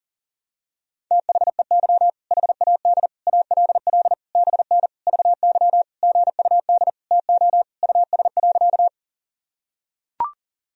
Morse code, heard as a keyed tone, about 40 wpm, 700 Hz. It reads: THEY HAD ALL BN VY GUD TO US.
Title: Sentences from Top 200 Words and Abbreviations 40wpm